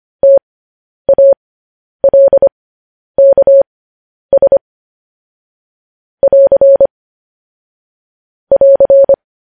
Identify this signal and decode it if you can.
Morse code, heard as a keyed tone, about 25 wpm, spaced as for 10 wpm, 550 Hz. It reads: TALKS <AR> <AR>
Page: news-headlines-in-morse-code-at-10-wpm